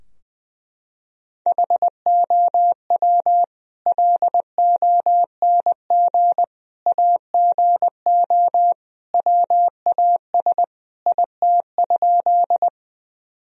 Morse code, heard as a keyed tone, about 21 wpm, 700 Hz. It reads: HOW LONG AGO WAS IT?
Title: Sentences from Top 1000 Words - Review Part 3 - 20wpm